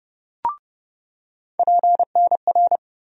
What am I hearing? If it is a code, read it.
Morse code, heard as a keyed tone, about 29 wpm, 700 Hz. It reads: PNR
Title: USA Call Sign Postfixes - 3 Letters 30wpm